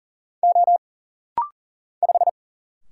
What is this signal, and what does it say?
Morse code, heard as a keyed tone, about 40 wpm, 700 Hz. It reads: O 5
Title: Single Letter-Number Alt - Rapid-Fire 40wpm